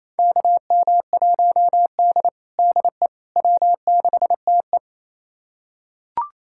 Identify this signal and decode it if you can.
Morse code, heard as a keyed tone, about 28 wpm, 700 Hz. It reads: KM1D DE W6TE
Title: USA Callsign DE Callsign - 4 Characters - 28wpm